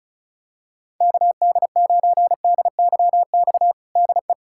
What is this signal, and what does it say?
Morse code, heard as a keyed tone, about 36 wpm, 700 Hz. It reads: KD9DYX DE
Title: USA Callsign DE Callsign 35wpm